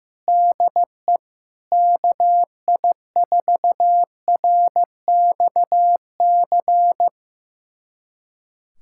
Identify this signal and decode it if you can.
Morse code, heard as a keyed tone, about 15 wpm, 700 Hz. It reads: DE KI4RXC
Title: USA Callsign DE Callsign 15wpm